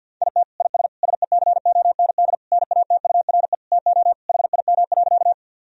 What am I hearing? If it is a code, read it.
Morse code, heard as a keyed tone, about 50 wpm, 700 Hz. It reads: IT IS SECOND NATURE TO HIM.
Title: Sentences from Top 1000 Words - Review Part 1 - 50wpm